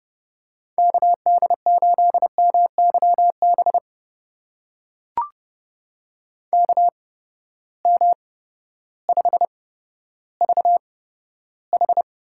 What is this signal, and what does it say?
Morse code, heard as a keyed tone, about 30 wpm, spaced as for 14 wpm, 700 Hz. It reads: KD8MYB KM5VH
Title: USA Call Signs - 8x Character Spacing - 30wpm